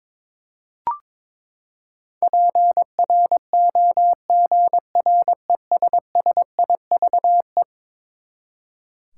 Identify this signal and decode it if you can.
Morse code, heard as a keyed tone, about 22 wpm, 700 Hz. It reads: PROGRESSIVE
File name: practice / 22wpm / Eleven Letter Words - 22wpm